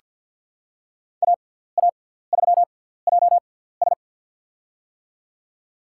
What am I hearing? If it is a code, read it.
Morse code, heard as a keyed tone, about 47 wpm, spaced as for 17 wpm, 700 Hz. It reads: AA3JS
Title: USA Call Signs - 6x Character Spacing - 50wpm